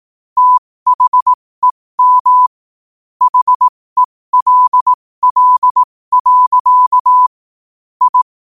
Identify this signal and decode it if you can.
Morse code, heard as a keyed tone, about 18 wpm, spaced as for 15 wpm, 1000 Hz. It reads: THEM HELL. I
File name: qotdfiles / 15wpm